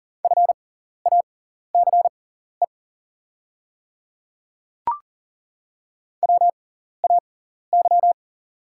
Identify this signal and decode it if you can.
Morse code, heard as a keyed tone, about 39 wpm, spaced as for 14 wpm, 690 Hz. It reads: FACE WAY
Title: 6x Character Spacing - Top 500 Words - 40wpm